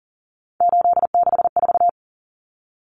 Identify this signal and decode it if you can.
Morse code, heard as a keyed tone, about 40 wpm, 700 Hz. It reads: 864
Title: Common Cut Numbers - 3-Digit 40wpm